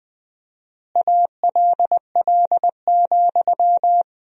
Morse code, heard as a keyed tone, about 20 wpm, 700 Hz. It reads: ALL,
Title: 4x Word-Spacing - Sentences from Top 500 Words - 20wpm